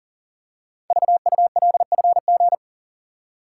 Morse code, heard as a keyed tone, about 39 wpm, 700 Hz. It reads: VULFG
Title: Code Groups - 5 Letters - 40wpm